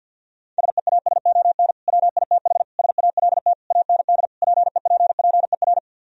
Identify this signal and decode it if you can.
Morse code, heard as a keyed tone, about 64 wpm, 700 Hz. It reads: SEASON WITH SALT AND PEPPER